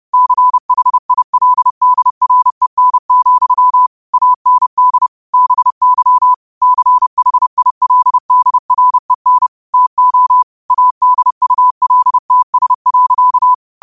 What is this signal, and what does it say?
Morse code, heard as a keyed tone, about 30 wpm, 1000 Hz. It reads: CHILDREN, AND BY CHILDREN TO ADULTS.
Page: Quotes for Sun, 17 Aug 2025 in Morse Code at 30 words per minute.